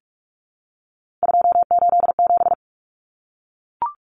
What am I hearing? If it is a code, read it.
Morse code, heard as a keyed tone, about 44 wpm, 700 Hz. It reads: 287
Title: Numbers - 3-Digit 45wpm